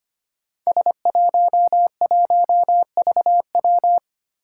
Morse code, heard as a keyed tone, about 25 wpm, 700 Hz. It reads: S11VW
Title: Five Challenging Letter-Number Combinations 25wpm